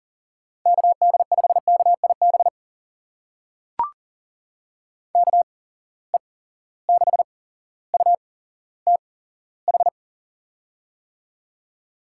Morse code, heard as a keyed tone, about 40 wpm, spaced as for 17 wpm, 700 Hz. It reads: KD5KIB KE6UTH
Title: USA Call Signs - 6 Characters - 8x Character Spacing - 40wpm